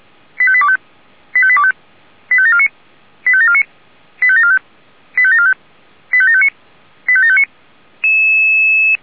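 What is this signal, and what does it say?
Five-tone selective call (ZVEI), heard as quick runs of five tones, tones of about 70 ms. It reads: 85825 85825 85749 85749 85745 85745 85759 85759
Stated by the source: ZVEI five-tone call received with an SDR.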